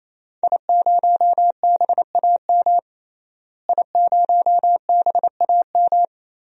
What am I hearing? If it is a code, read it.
Morse code, heard as a keyed tone, about 28 wpm, 700 Hz. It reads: I0BAM I0BAM
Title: DXCC-100 Call Signs - 5 Characters - Repeated 3x - 28wpm